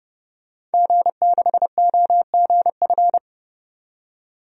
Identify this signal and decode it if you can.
Morse code, heard as a keyed tone, about 31 wpm, 700 Hz. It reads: G6OGF
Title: DXCC-100 Call Signs - 5 Characters - 30wpm